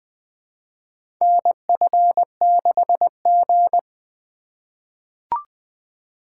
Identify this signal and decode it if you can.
Morse code, heard as a keyed tone, about 20 wpm, 700 Hz. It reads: NF6G
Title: USA Call Signs - 4 Characters - 2x Character Spacing - 20wpm